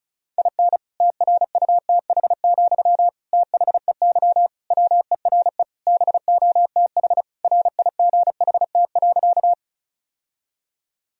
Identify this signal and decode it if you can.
Morse code, heard as a keyed tone, about 34 wpm, 700 Hz. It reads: IN TRUTH, THEY WERE BOTH RIGHT.
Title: Sentences from Top 1000 Words - Review Part 1 - 35wpm